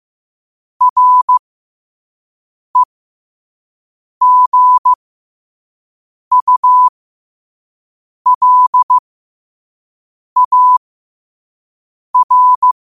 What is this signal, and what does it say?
Morse code, heard as a keyed tone, about 15 wpm, spaced as for 5 wpm, 1000 Hz. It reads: REGULAR
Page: Quotes for Fri, 15 Aug 2025 in Morse Code at 5 words per minute.